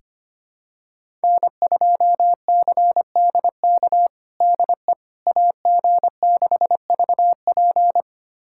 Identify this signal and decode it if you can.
Morse code, heard as a keyed tone, about 25 wpm, 700 Hz. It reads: N2CDK DE AG6VP
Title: USA Callsign DE Callsign - 5 Characters - 25wpm